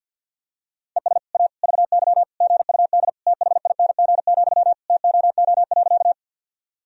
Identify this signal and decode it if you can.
Morse code, heard as a keyed tone, about 47 wpm, 700 Hz. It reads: ES A VY GUD THING, TOO.
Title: Sentences from Top 200 Words and Abbreviations 50wpm